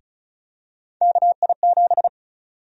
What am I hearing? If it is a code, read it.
Morse code, heard as a keyed tone, about 34 wpm, 700 Hz. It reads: KI7
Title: USA Call Sign Prefixes - Repeated 2x - 35wpm